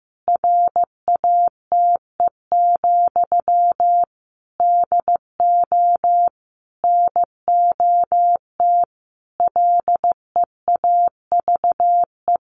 Morse code, heard as a keyed tone, about 15 wpm, 700 Hz. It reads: RATE, DO NOT LEAVE
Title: Sentences from Top 1500 Words 15wpm